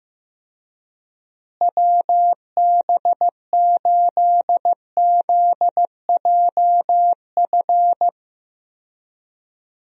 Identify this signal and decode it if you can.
Morse code, heard as a keyed tone, about 15 wpm, 700 Hz. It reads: WB8ZJF
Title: USA Call Signs - 6 Characters - Repeated 3x - 15wpm